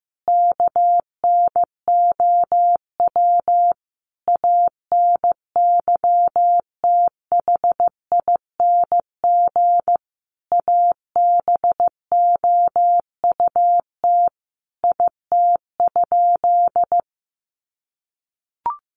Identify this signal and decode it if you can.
Morse code, heard as a keyed tone, about 15 wpm, 700 Hz. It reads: KNOW ANYTHING ABOUT IT?
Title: Sentences from Top 1000 Words 15wpm